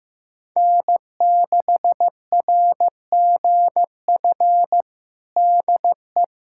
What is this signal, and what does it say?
Morse code, heard as a keyed tone, about 15 wpm, 700 Hz. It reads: N6RGF DE
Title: USA Callsign DE Callsign - 5 Characters - 15wpm